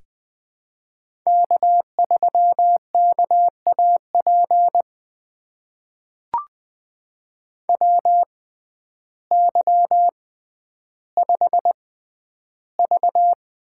Morse code, heard as a keyed tone, about 20 wpm, spaced as for 11 wpm, 700 Hz. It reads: K3KAP WY5V
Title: USA Call Signs - 6x Character Spacing - 20wpm